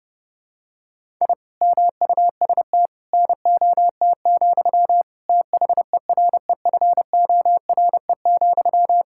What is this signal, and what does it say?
Morse code, heard as a keyed tone, about 30 wpm, 700 Hz. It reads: I MUST NOT, THEREFORE,
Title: Sentences from Top 1000 Words - Review Part 5 - 30wpm